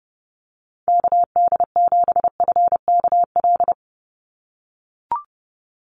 Morse code, heard as a keyed tone, about 30 wpm, 700 Hz. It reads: KD7FKL
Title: USA Call Signs 30wpm